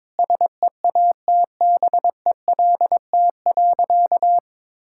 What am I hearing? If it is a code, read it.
Morse code, heard as a keyed tone, about 22 wpm, 700 Hz. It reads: SEATBELT.
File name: practice / 22wpm / Easy Sentences with Hints - 22wpm